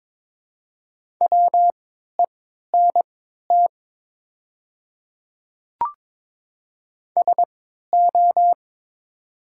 Morse code, heard as a keyed tone, about 22 wpm, spaced as for 13 wpm, 700 Hz. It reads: WENT SO
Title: 3x Character Spacing - Top 300 Words - 22wpm